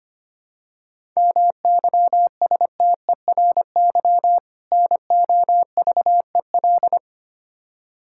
Morse code, heard as a keyed tone, about 25 wpm, 700 Hz. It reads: MYSTERY NOVEL
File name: practice / 25wpm / Difficult Words with Hints - 25wpm